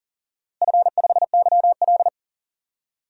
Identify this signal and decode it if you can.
Morse code, heard as a keyed tone, about 40 wpm, 700 Hz. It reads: F5YL
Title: DXCC-100 Call Signs - 4 Characters - Repeated 2x - 40wpm